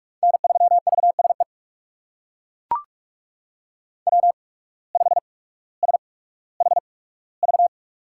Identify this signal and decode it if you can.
Morse code, heard as a keyed tone, about 42 wpm, spaced as for 19 wpm, 700 Hz. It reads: N3VSE W5SHV
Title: USA Call Signs - 5 Characters - 8x Character Spacing - 45wpm